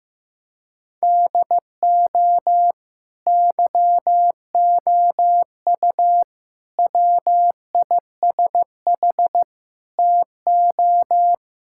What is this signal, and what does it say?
Morse code, heard as a keyed tone, about 15 wpm, 700 Hz. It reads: DO YOU WISH TO
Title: Sentences from Top 1600 Words 15wpm